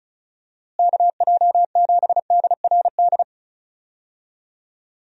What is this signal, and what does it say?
Morse code, heard as a keyed tone, about 34 wpm, 700 Hz. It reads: KJ7DRD
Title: USA Call Signs - 6 Characters - Repeated 2x - 35wpm